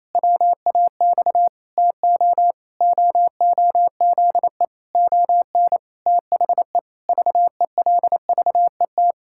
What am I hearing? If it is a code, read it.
Morse code, heard as a keyed tone, about 28 wpm, 700 Hz. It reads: WAX TO OOZE ON THE VELVET
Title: Pangram Challenge v2 - 28wpm